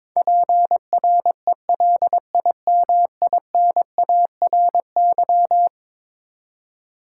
Morse code, heard as a keyed tone, about 22 wpm, 700 Hz. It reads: PRELIMINARY
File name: practice / 22wpm / Eleven Letter Words - 22wpm